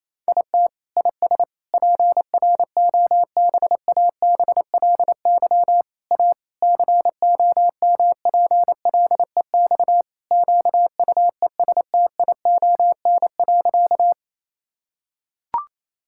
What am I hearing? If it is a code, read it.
Morse code, heard as a keyed tone, about 28 wpm, 700 Hz. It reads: IT IS PROBABLY A COMPLEX QUESTION.
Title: Sentences from Top 1000 Words - Review Part 6 - 28wpm